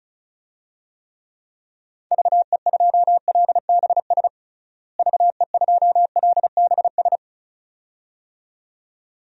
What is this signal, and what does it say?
Morse code, heard as a keyed tone, about 36 wpm, 700 Hz. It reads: VE2LBS VE2LBS
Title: DXCC-100 Call Signs - 6 Characters - Repeated 2x - 35wpm